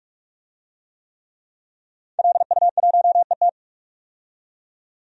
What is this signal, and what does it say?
Morse code, heard as a keyed tone, about 46 wpm, 700 Hz. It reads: LU1ET
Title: DXCC-100 Call Signs - 5 Characters - 45wpm